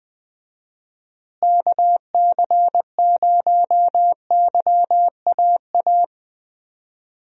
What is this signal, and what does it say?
Morse code, heard as a keyed tone, about 21 wpm, 700 Hz. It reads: KC0YAA
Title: USA Call Signs - 6 Characters - Repeated 3x - 20wpm